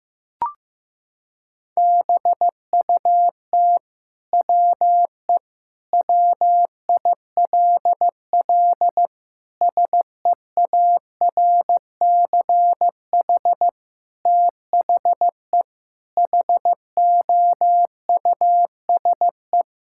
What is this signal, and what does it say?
Morse code, heard as a keyed tone, about 15 wpm, 700 Hz. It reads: BUT WE WILL SEARCH THE HOUSE
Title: Sentences from Top 1000 Words - Review Part 5 - 15wpm